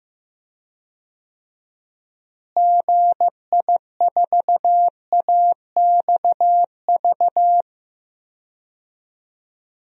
Morse code, heard as a keyed tone, about 15 wpm, 700 Hz. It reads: GI4AXV
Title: DXCC-100 Call Signs - 6 Characters - Repeated 3x - 15wpm